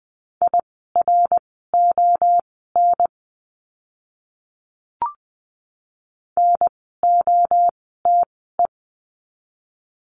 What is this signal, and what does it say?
Morse code, heard as a keyed tone, about 20 wpm, spaced as for 15 wpm, 700 Hz. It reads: IRON NOTE
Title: 054 Lesson 10f - Letter r - Words 20wpm